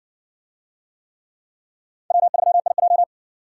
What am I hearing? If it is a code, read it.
Morse code, heard as a keyed tone, about 60 wpm, 700 Hz. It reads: W3IJ